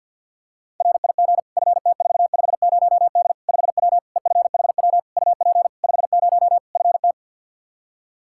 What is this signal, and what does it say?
Morse code, heard as a keyed tone, about 47 wpm, 700 Hz. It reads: RIG FT450D 5W EFHW UP 50 FT
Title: Practice QSO - Exchange 2 - Info - 50wpm